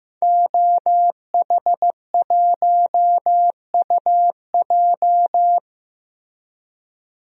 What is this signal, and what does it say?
Morse code, heard as a keyed tone, about 15 wpm, 700 Hz. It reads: OH1UJ
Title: DXCC-100 Call Signs - 5 Characters - Repeated 3x - 15wpm